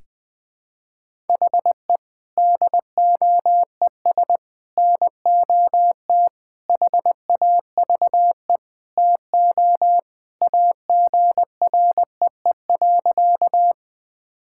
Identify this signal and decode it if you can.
Morse code, heard as a keyed tone, about 20 wpm, 700 Hz. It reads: HE DOES NOT HAVE TO AGREE.
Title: Sentences with 4-Word Phrases - Spoken Beginning - 20wpm